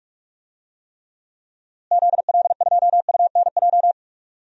Morse code, heard as a keyed tone, about 41 wpm, 700 Hz. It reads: ZL2UNJ
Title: DXCC-100 Call Signs - 6 Characters - Repeated 3x - 45wpm